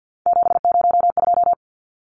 Morse code, heard as a keyed tone, about 50 wpm, 700 Hz. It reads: 702
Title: Numbers - 3-Digit 50wpm